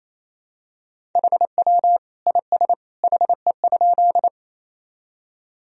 Morse code, heard as a keyed tone, about 28 wpm, 700 Hz. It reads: HW IS HE?
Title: Sentences from Top 100 Words and Abbreviations 28wpm